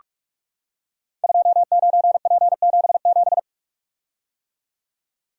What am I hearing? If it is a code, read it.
Morse code, heard as a keyed tone, about 44 wpm, 700 Hz. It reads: 29P77
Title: Code Groups - 5 Letter:Numbers - 45wpm